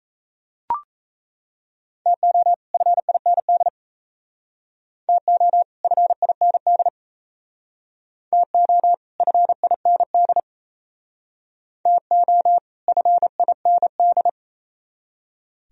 Morse code, heard as a keyed tone, about 36 wpm, 700 Hz. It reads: TO FIND TO FIND TO FIND TO FIND